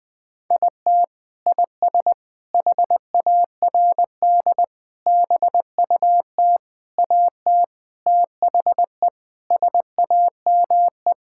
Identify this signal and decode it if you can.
Morse code, heard as a keyed tone, about 20 wpm, 700 Hz. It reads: IT IS HARD BUT AT THE SAME
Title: Sentences with 5-Word Phrases - Spoken Beginning - 20wpm